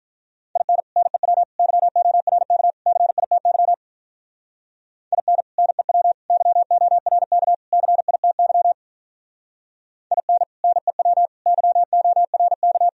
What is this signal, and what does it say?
Morse code, heard as a keyed tone, about 42 wpm, 700 Hz. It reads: IN NEW YORK CITY IN NEW YORK CITY IN NEW YORK
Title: Speed-Racing - Sets of 4 Words 35wpm